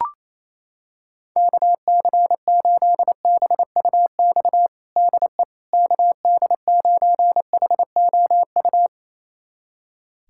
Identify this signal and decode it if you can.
Morse code, heard as a keyed tone, about 28 wpm, 700 Hz. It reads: KC8BUX DE KD9HOU
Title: USA Callsign DE Callsign - 6 Characters - 28wpm